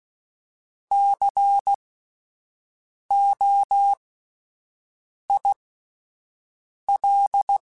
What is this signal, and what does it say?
Morse code, heard as a keyed tone, about 16 wpm, spaced as for 5 wpm, 800 Hz. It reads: COIL
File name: Spaced Alphas / Spaced Alphas mp3s / Cs sp @10